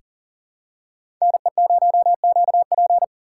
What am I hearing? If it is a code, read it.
Morse code, heard as a keyed tone, about 37 wpm, 700 Hz. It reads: NE0QP
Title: USA Call Signs - 5 Characters - 4x Character Spacing - 40wpm